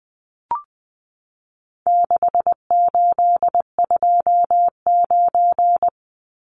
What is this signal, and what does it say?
Morse code, heard as a keyed tone, about 20 wpm, 700 Hz. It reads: 6829
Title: Numbers - 4-Digit 20wpm